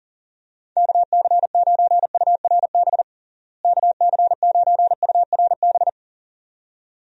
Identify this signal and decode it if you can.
Morse code, heard as a keyed tone, about 37 wpm, 700 Hz. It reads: KC9URB KC9URB
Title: USA Call Signs - 6 Characters - Repeated 2x - 40wpm